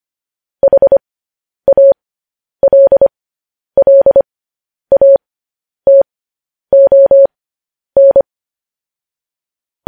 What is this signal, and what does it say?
Morse code, heard as a keyed tone, about 25 wpm, spaced as for 10 wpm, 550 Hz. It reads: HALLATON
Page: news headlines in morse code at 10 wpm